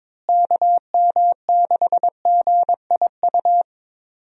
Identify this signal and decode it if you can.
Morse code, heard as a keyed tone, about 22 wpm, 700 Hz. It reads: KM6GIU
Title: USA Call Signs - 6 Characters - Repeated 2x - 22wpm